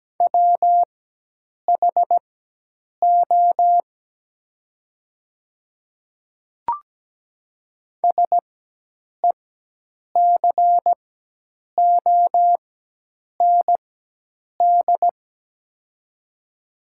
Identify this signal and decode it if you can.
Morse code, heard as a keyed tone, about 17 wpm, spaced as for 8 wpm, 700 Hz. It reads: WHO SECOND
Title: 4x Character Spacing - Top 300 Words - 17wpm